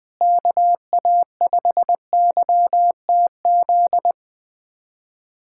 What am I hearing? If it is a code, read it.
Morse code, heard as a keyed tone, about 20 wpm, 700 Hz. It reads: KA5YTZ
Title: USA Callsign DE Callsign - 6 Characters - 20wpm